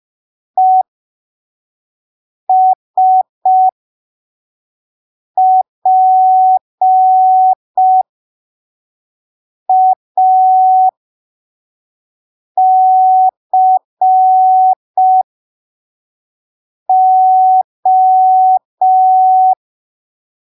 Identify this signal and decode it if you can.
Morse code, heard as a keyed tone, about 5 wpm, spaced as for 3 wpm, 750 Hz. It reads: ESPACO
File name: arquivos / texto / texto 21 05ppm